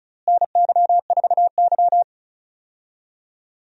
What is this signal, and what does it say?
Morse code, heard as a keyed tone, about 36 wpm, 700 Hz. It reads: NY4Y
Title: USA Call Signs - 4 Characters - Repeated 3x - 35wpm